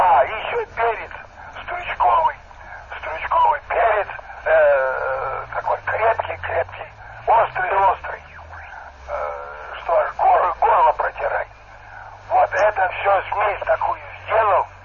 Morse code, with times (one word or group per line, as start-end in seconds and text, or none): none